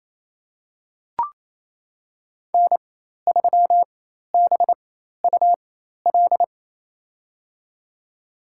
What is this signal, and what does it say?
Morse code, heard as a keyed tone, about 27 wpm, spaced as for 13 wpm, 700 Hz. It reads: N3BUL
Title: USA Call Signs - 5 Characters - 4x Character Spacing - 28wpm